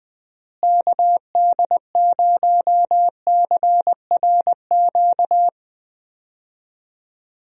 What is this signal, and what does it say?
Morse code, heard as a keyed tone, about 20 wpm, 700 Hz. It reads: KD0CRQ
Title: USA Call Signs - 6 Characters - Repeated 3x - 20wpm